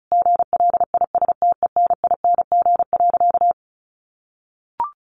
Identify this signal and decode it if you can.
Morse code, heard as a keyed tone, about 34 wpm, 700 Hz. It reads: GLISTENING.
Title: A Sound Of Thunder 35wpm